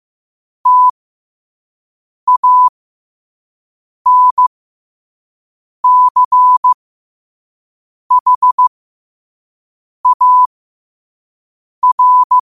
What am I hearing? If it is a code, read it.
Morse code, heard as a keyed tone, about 15 wpm, spaced as for 5 wpm, 1000 Hz. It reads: TANCHAR
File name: 5wpm